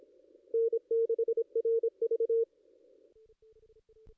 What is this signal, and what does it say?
Morse code, heard as a keyed tone, about 27 wpm, 460 Hz. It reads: N6RV
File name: contests / ss-cw-k5tr-op-n5rz -2200